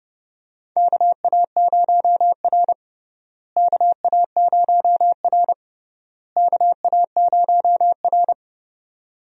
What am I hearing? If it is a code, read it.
Morse code, heard as a keyed tone, about 30 wpm, 700 Hz. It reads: KA0R KA0R KA0R
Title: USA Call Signs - 4 Characters - Repeated 3x - 30wpm